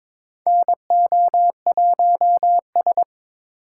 Morse code, heard as a keyed tone, about 22 wpm, 700 Hz. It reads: NO1S
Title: USA Call Signs - 4 Characters - 4x Character Spacing - 22wpm